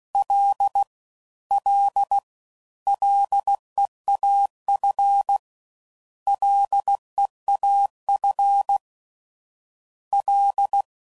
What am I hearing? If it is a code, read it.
Morse code, heard as a keyed tone, about 16 wpm, 800 Hz. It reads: L L LEAF LEAF L